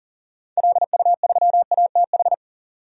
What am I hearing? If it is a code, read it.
Morse code, heard as a keyed tone, about 39 wpm, 700 Hz. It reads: LU3ATH
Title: DXCC-100 Call Signs - 6 Characters - 40wpm